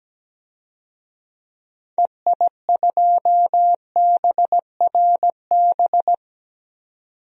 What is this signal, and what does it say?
Morse code, heard as a keyed tone, about 17 wpm, 700 Hz. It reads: EI2BRB
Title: DXCC-100 Call Signs - 6 Characters - 17wpm